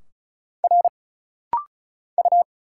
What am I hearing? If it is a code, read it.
Morse code, heard as a keyed tone, about 36 wpm, 700 Hz. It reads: R U
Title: Single Letters - Rapid-Fire 35wpm